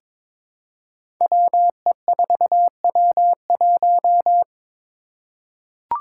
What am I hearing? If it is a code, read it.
Morse code, heard as a keyed tone, about 22 wpm, 700 Hz. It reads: WE4W1
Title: Five Challenging Letter-Number Combinations 22wpm